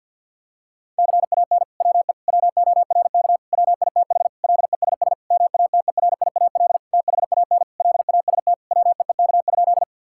Morse code, heard as a keyed tone, about 52 wpm, 700 Hz. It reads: CAN WE WORK WITH LESS MATERIAL THAN LAST WEEK?
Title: Sentences from Top 1000 Words - Review Part 5 - 50wpm